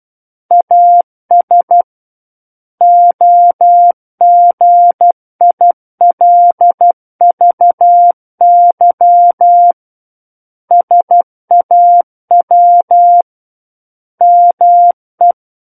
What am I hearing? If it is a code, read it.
Morse code, heard as a keyed tone, about 12 wpm, 700 Hz. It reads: AS OGILVY SAW ME
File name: Book One / 03-Chapter 3 - 12 WPM